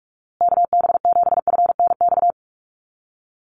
Morse code, heard as a keyed tone, about 44 wpm, 690 Hz. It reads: KB7FNX
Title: USA Call Signs Repeated 2x 45wpm